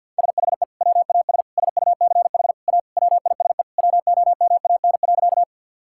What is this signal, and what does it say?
Morse code, heard as a keyed tone, about 51 wpm, 700 Hz. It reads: SHE WAS SUCH A WISE WOMAN.
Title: Sentences from Top 1600 Words 50wpm